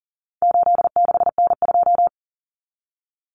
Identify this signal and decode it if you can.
Morse code, heard as a keyed tone, about 40 wpm, 700 Hz. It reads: 86N2
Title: Common Cut Numbers - 4-Digit 40wpm